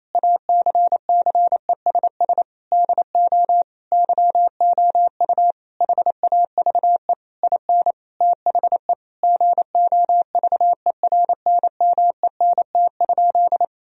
Morse code, heard as a keyed tone, about 28 wpm, 700 Hz. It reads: ACCESS DO YOU HAVE IN THE GOVERNMENT?
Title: Sentences from Top 1000 Words - Review Part 4 - 28wpm